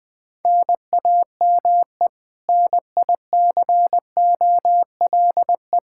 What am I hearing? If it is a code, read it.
Morse code, heard as a keyed tone, about 20 wpm, 700 Hz. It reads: NAME NICOLE
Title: Practice QSO - Exchange 1 - Beginning - 20wpm